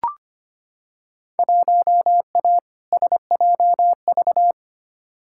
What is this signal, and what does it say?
Morse code, heard as a keyed tone, about 25 wpm, 700 Hz. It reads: 1A SJV
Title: Field Day Exchanges Encore - 25wpm